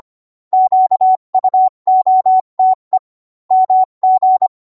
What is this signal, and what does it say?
Morse code, heard as a keyed tone, about 25 wpm, 750 Hz. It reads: QUOTE MG